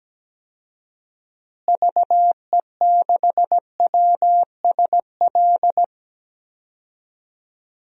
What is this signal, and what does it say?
Morse code, heard as a keyed tone, about 17 wpm, 700 Hz. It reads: VE6WSL
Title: DXCC-100 Call Signs - 6 Characters - Repeated 2x - 17wpm